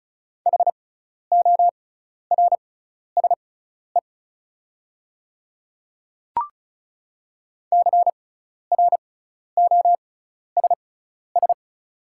Morse code, heard as a keyed tone, about 36 wpm, spaced as for 12 wpm, 700 Hz. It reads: HORSE CROSS
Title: 6x Character Spacing - Top 300 Words - 35wpm